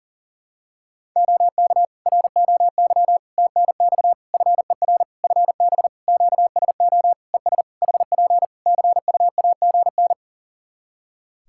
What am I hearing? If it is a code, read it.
Morse code, heard as a keyed tone, about 40 wpm, 700 Hz. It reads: OK ROY TNX FER FB QSO ES HP CUAGN
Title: Practice QSO - Exchange 4 - Ending - 40wpm